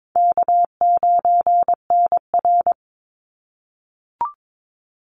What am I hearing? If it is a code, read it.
Morse code, heard as a keyed tone, about 22 wpm, 700 Hz. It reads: K9NR
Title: Easy Call Signs - 22wpm